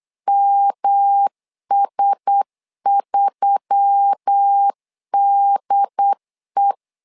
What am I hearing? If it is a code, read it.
Morse code, heard as a keyed tone, about 8 wpm, 800 Hz. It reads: MS3DE
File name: LEZIONI / 4 Q1S3 8wpm 3037